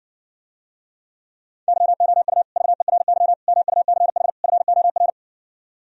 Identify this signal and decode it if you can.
Morse code, heard as a keyed tone, about 60 wpm, 700 Hz. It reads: YOU VERY MUCH FOR